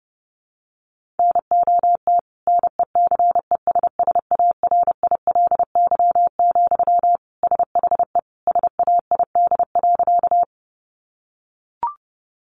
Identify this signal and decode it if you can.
Morse code, heard as a keyed tone, about 30 wpm, 700 Hz. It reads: NOT NECESSARILY, SHE SAID.
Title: Sentences from Top 1400 Words 30wpm